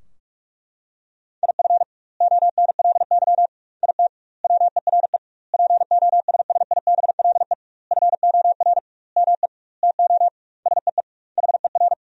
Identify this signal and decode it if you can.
Morse code, heard as a keyed tone, about 41 wpm, 700 Hz. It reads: IF ONLY IT WERE POSSIBLE FOR ME TO SEE HER
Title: 2x Word-Spacing - Sentences from Top 500 Words - 45wpm